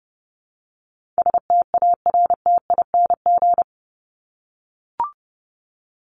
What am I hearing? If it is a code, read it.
Morse code, heard as a keyed tone, about 30 wpm, 700 Hz. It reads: STARTING
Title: ING Suffix - Top 100 Words with Base Words 30wpm